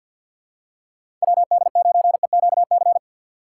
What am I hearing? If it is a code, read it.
Morse code, heard as a keyed tone, about 53 wpm, 700 Hz. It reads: WD9EQC
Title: USA Call Signs - 6 Characters - 3x Character Spacing - 50wpm